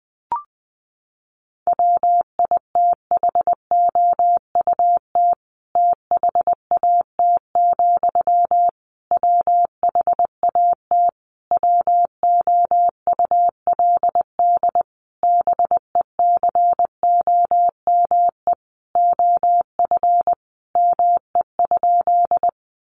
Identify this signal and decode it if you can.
Morse code, heard as a keyed tone, about 20 wpm, 700 Hz. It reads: WITHOUT THAT, WHAT WOULD BECOME OF ME?
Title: Sentences from Top 600 Words 20wpm